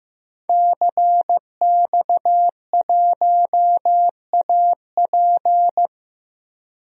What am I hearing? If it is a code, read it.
Morse code, heard as a keyed tone, about 15 wpm, 700 Hz. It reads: CX1AP
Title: DXCC-100 Call Signs - 5 Characters - Repeated 2x - 15wpm